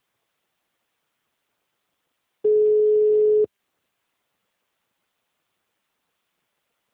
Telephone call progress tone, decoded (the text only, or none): ringback tone